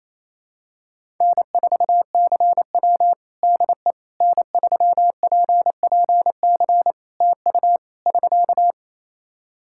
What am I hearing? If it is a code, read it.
Morse code, heard as a keyed tone, about 28 wpm, 700 Hz. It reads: N4CW DE N3PPC TU <SK>
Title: Practice QSO - Exchange 4 - Ending - 28wpm